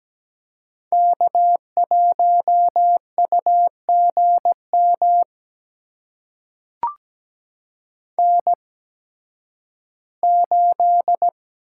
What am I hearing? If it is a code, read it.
Morse code, heard as a keyed tone, about 17 wpm, spaced as for 11 wpm, 700 Hz. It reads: K1UGM N8
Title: USA Call Signs - 5 Characters - 8x Character Spacing - 17wpm